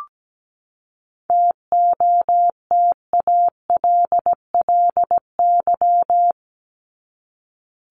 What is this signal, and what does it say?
Morse code, heard as a keyed tone, about 17 wpm, 700 Hz. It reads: TOTALLY
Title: LY Suffix - Top 100 Words with Base Words 17wpm